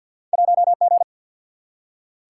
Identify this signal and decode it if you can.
Morse code, heard as a keyed tone, about 48 wpm, 700 Hz. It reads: 1 G
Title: Single-Letter Single-Number 50wpm